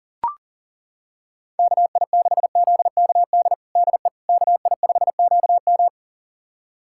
Morse code, heard as a keyed tone, about 40 wpm, 700 Hz. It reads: KI6ZKD DE KI5QM
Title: USA Callsign DE Callsign 40wpm